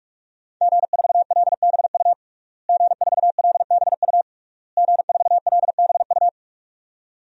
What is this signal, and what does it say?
Morse code, heard as a keyed tone, about 46 wpm, 700 Hz. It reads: G4LBU G4LBU G4LBU
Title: DXCC-100 Call Signs - 5 Characters - Repeated 3x - 45wpm